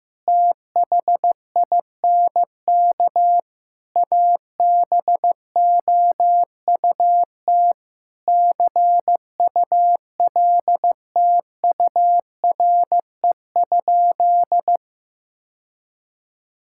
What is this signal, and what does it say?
Morse code, heard as a keyed tone, about 15 wpm, 700 Hz. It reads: THINK ABOUT CULTURE?
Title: Sentences from Top 1000 Words - Review Part 6 - 15wpm